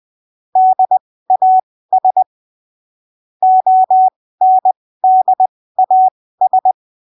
Morse code, heard as a keyed tone, about 20 wpm, spaced as for 15 wpm, 750 Hz. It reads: DAS ONDAS